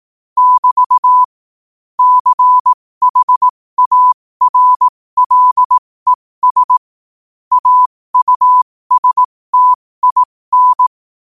Morse code, heard as a keyed tone, about 18 wpm, spaced as for 15 wpm, 1000 Hz. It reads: <BT> CHARLES AUSTIN
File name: qotdfiles / 15wpm